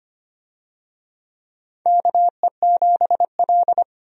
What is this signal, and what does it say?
Morse code, heard as a keyed tone, about 25 wpm, 700 Hz. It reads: KE7L
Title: USA Call Signs - 4 Characters - 25wpm